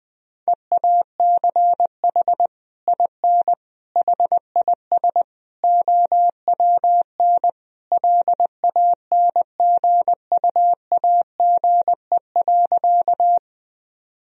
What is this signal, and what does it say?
Morse code, heard as a keyed tone, about 20 wpm, 700 Hz. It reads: EACH IN HIS OWN LANGUAGE.
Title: Sentences from Top 1000 Words - Review Part 3 - 20wpm